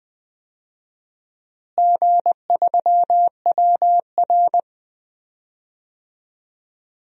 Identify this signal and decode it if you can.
Morse code, heard as a keyed tone, about 20 wpm, 690 Hz. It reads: G3WR
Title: DXCC-100 Call Signs - 4 Characters - Repeated 3x - 20wpm